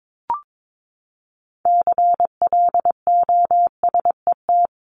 Morse code, heard as a keyed tone, about 22 wpm, 700 Hz. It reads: CLOSET
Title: Top 1751-2000 Words 22wpm